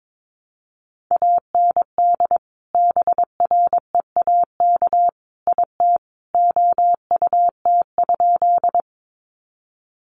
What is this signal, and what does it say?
Morse code, heard as a keyed tone, about 22 wpm, 700 Hz. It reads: AND BREAK IT OUT?
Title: Sentences from Top 900 Words 22wpm